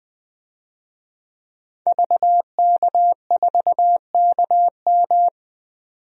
Morse code, heard as a keyed tone, about 20 wpm, 700 Hz. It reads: VK4KM
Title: DXCC-100 Call Signs - 5 Characters - 20wpm